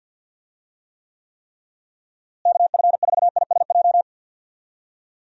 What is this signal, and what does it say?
Morse code, heard as a keyed tone, about 52 wpm, 700 Hz. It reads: KF4ISJ
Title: USA Call Signs - 6 Characters - Repeated 3x - 50wpm